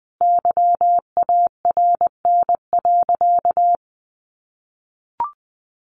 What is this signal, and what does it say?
Morse code, heard as a keyed tone, about 20 wpm, 700 Hz. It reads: YARN.
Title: Additional Easy Phrases 20wpm